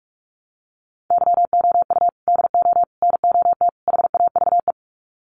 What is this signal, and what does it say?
Morse code, heard as a keyed tone, about 44 wpm, 700 Hz. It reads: YOU DO NOT HAVE
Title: Sets of 4 Words 45wpm